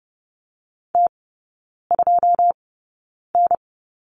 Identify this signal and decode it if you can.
Morse code, heard as a keyed tone, about 30 wpm, spaced as for 9 wpm, 700 Hz. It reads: T2N
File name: 081 Lesson 14e - Number 2 - Three Characters 30wpm